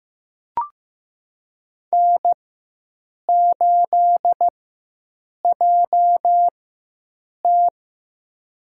Morse code, heard as a keyed tone, about 15 wpm, spaced as for 7 wpm, 700 Hz. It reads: N8JT
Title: USA Call Signs - 4 Characters - 4x Character Spacing - 15wpm